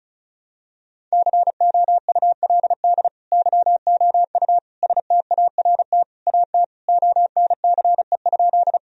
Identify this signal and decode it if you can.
Morse code, heard as a keyed tone, about 36 wpm, 700 Hz. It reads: COULD YOU START AT ONCE?
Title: Sentences from Top 1000 Words - Review Part 1 - 35wpm